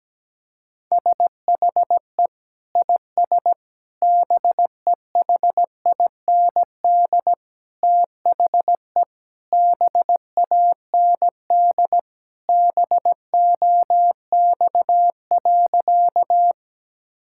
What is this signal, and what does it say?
Morse code, heard as a keyed tone, about 17 wpm, 700 Hz. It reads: SHE IS BEHIND THE BAND BOX.
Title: Sentences from Top 1600 Words 17wpm